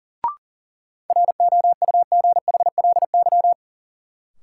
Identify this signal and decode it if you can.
Morse code, heard as a keyed tone, about 40 wpm, 700 Hz. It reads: ROUGHLY
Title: Sentences from Top 1500 Words - New Word Drill Alt - 40wpm